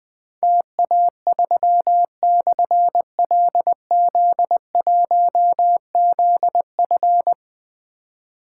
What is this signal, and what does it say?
Morse code, heard as a keyed tone, about 21 wpm, 700 Hz. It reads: TA3/LZ1ZF
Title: Difficult Call Signs 20wpm